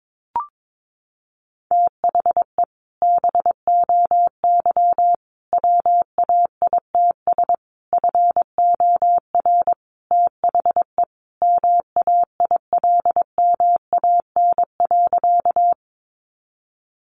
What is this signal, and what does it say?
Morse code, heard as a keyed tone, about 22 wpm, 700 Hz. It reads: THE BOY WAITS FOR THE MAILMAN.
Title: Easy Phrases 22wpm